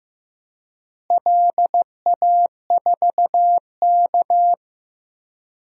Morse code, heard as a keyed tone, about 16 wpm, 700 Hz. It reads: LA4K
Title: DXCC-100 Call Signs - 4 Characters - Repeated 3x - 15wpm